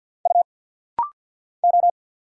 Morse code, heard as a keyed tone, about 50 wpm, 700 Hz. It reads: U O
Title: Single Letters - Rapid-Fire 50wpm